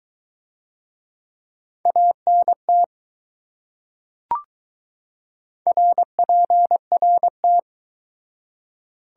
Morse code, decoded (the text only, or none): ANT RPRT